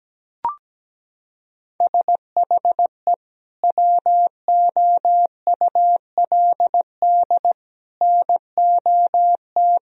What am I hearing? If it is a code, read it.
Morse code, heard as a keyed tone, about 17 wpm, 700 Hz. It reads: SHE WOULD NOT
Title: Sentences from Top 1000 Words - Review Part 3 - 17wpm